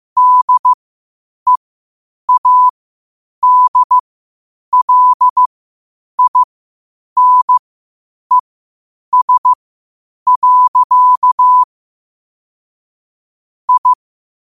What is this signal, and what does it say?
Morse code, heard as a keyed tone, about 15 wpm, spaced as for 8 wpm, 1000 Hz. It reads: DEADLINES. I
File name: qotdfiles / 8wpm